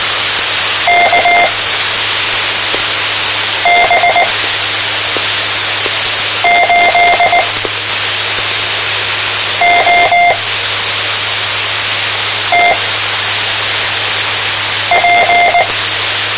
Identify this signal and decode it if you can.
Morse code, heard as a keyed tone, about 19 wpm, spaced as for 4 wpm, 700 Hz, with noise. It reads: KB8OTP
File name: tmranets / Elmer 04 19 2015